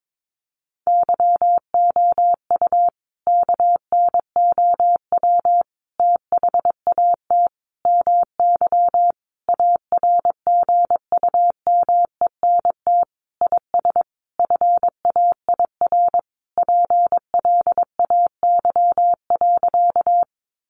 Morse code, heard as a keyed tone, about 22 wpm, 700 Hz. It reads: YOU KNOW THAT MY ARGUMENT IS FAIR PLAY.
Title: Sentences from Top 1400 Words 22wpm